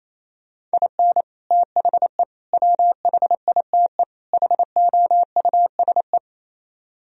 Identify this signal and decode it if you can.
Morse code, heard as a keyed tone, about 37 wpm, 700 Hz. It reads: IN THE WHITE HOUSE